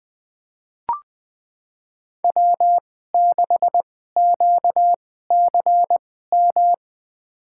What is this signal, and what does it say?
Morse code, heard as a keyed tone, about 20 wpm, spaced as for 15 wpm, 700 Hz. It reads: W6QCM
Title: USA Call Signs - 5 Characters - 2x Character Spacing - 20wpm